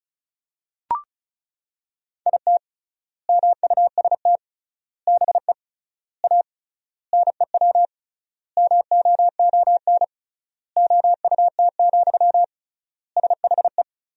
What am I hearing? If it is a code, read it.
Morse code, heard as a keyed tone, about 34 wpm, 700 Hz. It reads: IT MUST BE A NEW MOON OUT, SHE
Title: 3x Word-Spacing - Sentences from Top 500 Words - 35wpm